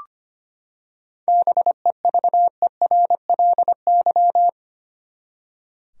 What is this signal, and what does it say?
Morse code, heard as a keyed tone, about 25 wpm, 700 Hz. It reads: BEVERLY
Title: USA Common Names - 25wpm